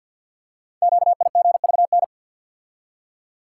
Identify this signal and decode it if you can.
Morse code, heard as a keyed tone, about 48 wpm, 700 Hz. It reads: QIGVN
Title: Code Groups - 5 Letters - 50wpm